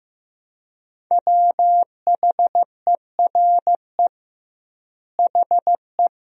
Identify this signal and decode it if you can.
Morse code, heard as a keyed tone, about 15 wpm, 700 Hz. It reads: WHERE HE
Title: 2x Word-Spacing - Sentences from Top 400 Words - 15wpm